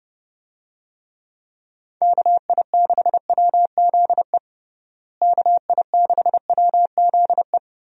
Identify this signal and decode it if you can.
Morse code, heard as a keyed tone, about 30 wpm, 700 Hz. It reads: KI6WZE KI6WZE
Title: USA Call Signs - 6 Characters - Repeated 2x - 30wpm